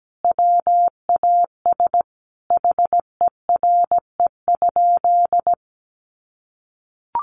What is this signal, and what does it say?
Morse code, heard as a keyed tone, about 17 wpm, 700 Hz. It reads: WAS HERE?
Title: Sentences from Top 1500 Words 17wpm